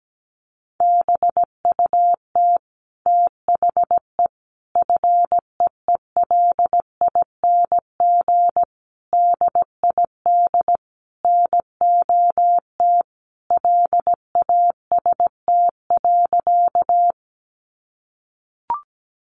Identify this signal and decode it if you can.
Morse code, heard as a keyed tone, about 17 wpm, 700 Hz. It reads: BUT THE FEELING DID NOT LAST.
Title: Sentences from Top 1200 Words 17wpm